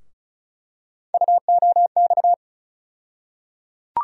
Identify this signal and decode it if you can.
Morse code, heard as a keyed tone, about 36 wpm, 700 Hz. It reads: UOX
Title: USA Call Sign Postfixes - 3 Letters 35wpm